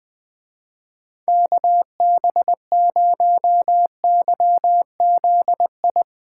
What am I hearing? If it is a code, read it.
Morse code, heard as a keyed tone, about 20 wpm, 700 Hz. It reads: KB0YZI